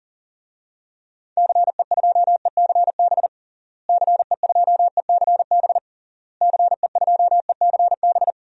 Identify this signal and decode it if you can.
Morse code, heard as a keyed tone, about 37 wpm, 700 Hz. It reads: CE2ECB CE2ECB CE2ECB
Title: DXCC-100 Call Signs - 6 Characters - Repeated 3x - 40wpm